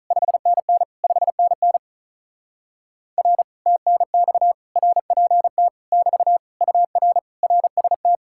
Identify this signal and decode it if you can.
Morse code, heard as a keyed tone, about 48 wpm, 700 Hz. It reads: 5NN 5NN R TNX RPT <BT> UR RST